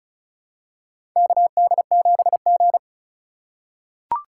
Morse code, heard as a keyed tone, about 36 wpm, 700 Hz. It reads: KD7G
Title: USA Call Signs - 4 Characters - 2x Character Spacing - 35wpm